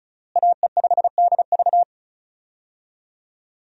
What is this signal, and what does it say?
Morse code, heard as a keyed tone, about 34 wpm, 700 Hz. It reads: AE5DV
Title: USA Call Signs - 5 Characters - Repeated 2x - 35wpm